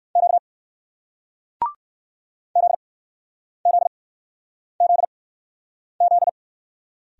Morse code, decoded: Q ZZZZ

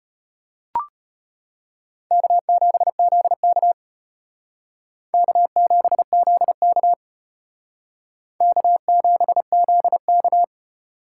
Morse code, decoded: K7ZK K7ZK K7ZK

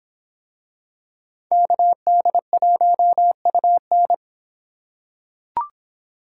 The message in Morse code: KD1UN

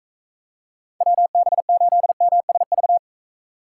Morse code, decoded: WB8MSV

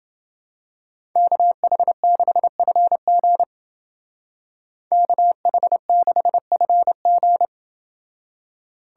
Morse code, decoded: KH6FG KH6FG

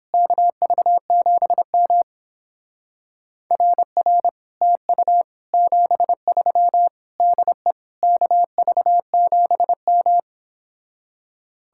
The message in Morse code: KV7M RR TU 73 DE KV7M

